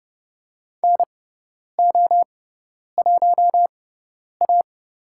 Morse code, heard as a keyed tone, about 30 wpm, spaced as for 10 wpm, 700 Hz. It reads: NO1A
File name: Lesson 16g - Letter c - Call Signs 30wpm